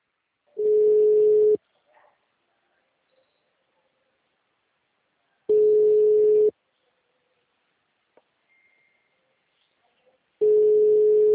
Ringback tone (Europe, 425 Hz).
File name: out- -2016- - - .18